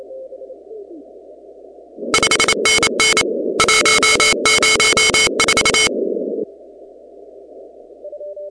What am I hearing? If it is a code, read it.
Morse code, heard as a keyed tone, about 28 wpm, 2200 Hz, with noise. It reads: 5NN 104
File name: contests / 2019-cw-wpx-kt5j-k5pi- -2330 0-17